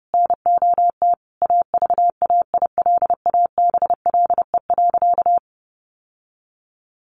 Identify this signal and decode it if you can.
Morse code, heard as a keyed tone, about 30 wpm, 700 Hz. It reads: NOT AVAILABLE.
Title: Sentences from Top 600 Words 30wpm